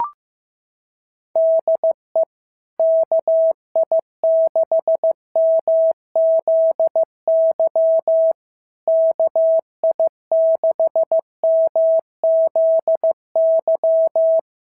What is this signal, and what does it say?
Morse code, heard as a keyed tone, about 15 wpm, 650 Hz. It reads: DE KI6MZY KI6MZY
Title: SOTA QSO Practice 15wpm